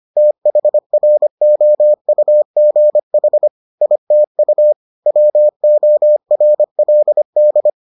Morse code, heard as a keyed tone, about 25 wpm, 600 Hz. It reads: THROUGH ITU WORLD